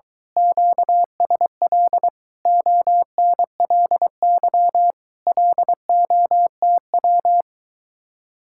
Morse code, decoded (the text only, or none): QSL ONLY LOTW